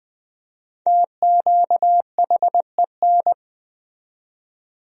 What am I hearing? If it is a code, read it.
Morse code, heard as a keyed tone, about 20 wpm, 700 Hz. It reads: TQHEN
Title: 4x Word-Spacing - Sentences from Top 100 Words - 20wpm